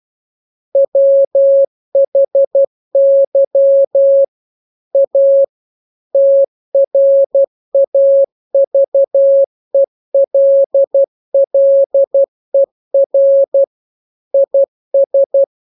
Morse code, heard as a keyed tone, about 12 wpm, 550 Hz. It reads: WHY A TRAVELLER IS